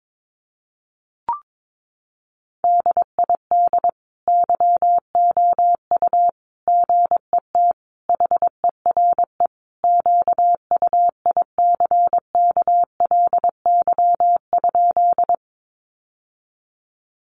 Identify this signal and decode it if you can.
Morse code, heard as a keyed tone, about 22 wpm, 700 Hz. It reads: DID YOU GET HERE QUICKLY?
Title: Sentences from Top 700 Words 22wpm